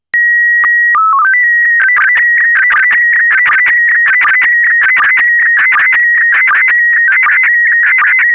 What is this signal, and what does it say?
SSTV (slow-scan television), PD 180.
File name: index06 / segnalidigitali / SSTV PD180